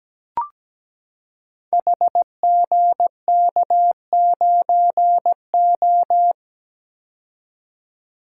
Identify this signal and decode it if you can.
Morse code, heard as a keyed tone, about 17 wpm, 700 Hz. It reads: HGK9O
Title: Code Groups - 5 Letter:Numbers - 17wpm